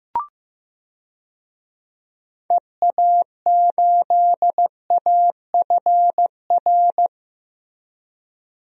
Morse code, heard as a keyed tone, about 15 wpm, 700 Hz. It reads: EA8AFR
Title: DXCC-100 Call Signs - 6 Characters - 15wpm